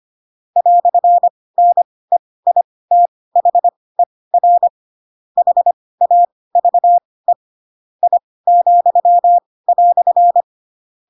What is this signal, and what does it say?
Morse code, heard as a keyed tone, about 25 wpm, spaced as for 18 wpm, 700 Hz. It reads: "NEITHER HAVE I,"